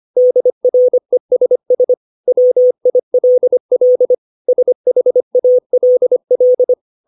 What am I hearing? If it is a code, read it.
Morse code, heard as a keyed tone, about 25 wpm, 500 Hz. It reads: DRESS WILL SHALL
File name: 632 words / 632 Words 25WPM